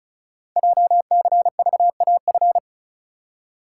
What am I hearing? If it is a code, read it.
Morse code, heard as a keyed tone, about 36 wpm, 700 Hz. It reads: JCVAF
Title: Code Groups - 5 Letters - 35wpm